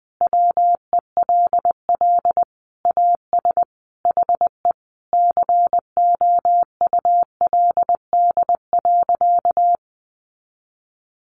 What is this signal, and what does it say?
Morse code, heard as a keyed tone, about 20 wpm, 700 Hz. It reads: WELL AS HE COULD.
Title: Sentences from Top 500 Words 20wpm